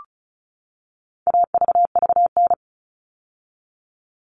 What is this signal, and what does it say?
Morse code, heard as a keyed tone, about 34 wpm, 700 Hz. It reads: AVVN
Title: All Cut Numbers - 4-Digit 35wpm